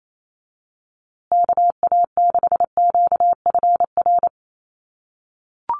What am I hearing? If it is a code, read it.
Morse code, heard as a keyed tone, about 28 wpm, 700 Hz. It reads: KA6QFR
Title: USA Call Signs Repeated 3x 28wpm